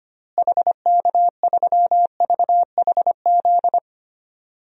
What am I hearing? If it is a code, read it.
Morse code, heard as a keyed tone, about 25 wpm, 700 Hz. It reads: HK3VHZ
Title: DXCC-100 Call Signs - 6 Characters - Repeated 2x - 25wpm